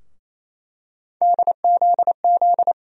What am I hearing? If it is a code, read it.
Morse code, heard as a keyed tone, about 28 wpm, 700 Hz. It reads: DZZ
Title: USA Call Sign Postfixes - 3 Letters 28wpm